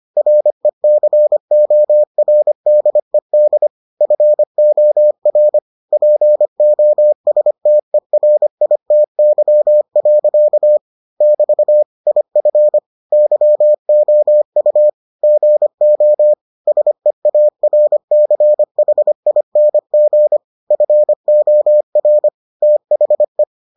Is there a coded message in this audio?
Yes: Morse code from 0.2 to 23.4 s, heard as a keyed tone, about 25 wpm, 600 Hz. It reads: RECORDED FOR POSTERITY. <BT> IF YOU GO SEARCHING FOR THE